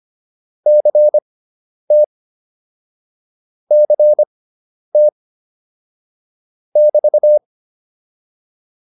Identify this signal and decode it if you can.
Morse code, heard as a keyed tone, about 25 wpm, spaced as for 10 wpm, 600 Hz. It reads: CT CT <BT>
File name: QSO206 10